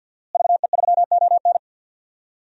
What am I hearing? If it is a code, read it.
Morse code, heard as a keyed tone, about 48 wpm, 700 Hz. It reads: VE3ON
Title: DXCC-100 Call Signs - 5 Characters - Repeated 2x - 50wpm